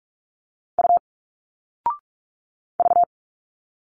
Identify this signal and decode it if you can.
Morse code, heard as a keyed tone, about 46 wpm, 700 Hz. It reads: U V
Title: Single Letters 45wpm